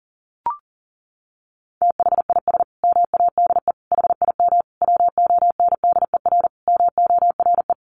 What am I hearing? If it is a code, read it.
Morse code, heard as a keyed tone, about 40 wpm, 700 Hz. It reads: THIS MADE HIM WONDER MORE
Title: Sentences from Top 500 Words 40wpm